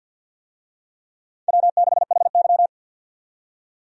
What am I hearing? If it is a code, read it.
Morse code, heard as a keyed tone, about 48 wpm, 700 Hz. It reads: W6HY
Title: USA Call Signs - 4 Characters - 50wpm